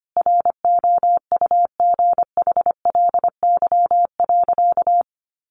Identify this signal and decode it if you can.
Morse code, heard as a keyed tone, about 25 wpm, 700 Hz. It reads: ROUGHLY.
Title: Sentences from Top 1500 Words 25wpm